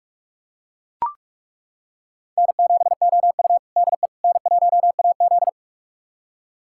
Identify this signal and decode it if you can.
Morse code, heard as a keyed tone, about 41 wpm, 700 Hz. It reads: N7OU DE N1AZ
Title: USA Callsign DE Callsign - 4 Characters - 45wpm